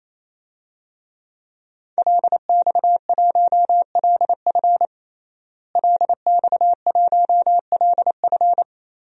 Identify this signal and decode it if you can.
Morse code, heard as a keyed tone, about 28 wpm, 700 Hz. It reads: LX1LF LX1LF
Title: DXCC-100 Call Signs - 5 Characters - Repeated 3x - 28wpm